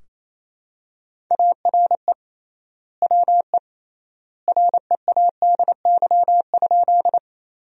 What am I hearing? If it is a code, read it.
Morse code, heard as a keyed tone, about 28 wpm, 700 Hz. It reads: ARE WE READY?
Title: 3x Word-Spacing - Sentences from Top 400 Words - 28wpm